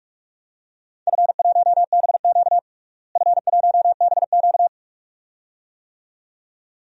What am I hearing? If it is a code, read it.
Morse code, heard as a keyed tone, about 41 wpm, 700 Hz. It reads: F1BQ F1BQ
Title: DXCC-100 Call Signs - 4 Characters - Repeated 2x - 45wpm